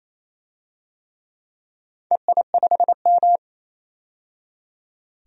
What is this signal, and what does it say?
Morse code, heard as a keyed tone, about 27 wpm, 700 Hz. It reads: EI5M
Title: DXCC-100 Call Signs - 4 Characters - 28wpm